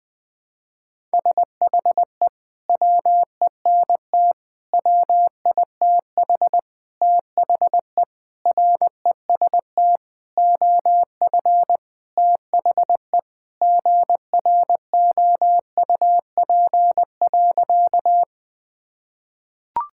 Morse code, heard as a keyed tone, about 21 wpm, 700 Hz. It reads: SHE WENT WITH THE REST OF THE GROUP.
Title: Sentences with 5-Word Phrases - 20wpm